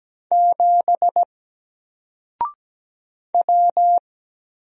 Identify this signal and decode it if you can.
Morse code, heard as a keyed tone, about 17 wpm, 700 Hz. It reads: 7 W
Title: Single Letter-Number 17wpm